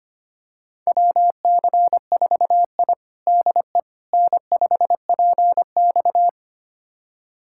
Morse code, heard as a keyed tone, about 25 wpm, 700 Hz. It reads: WC4I DE N5PX
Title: USA Callsign DE Callsign - 4 Characters - 25wpm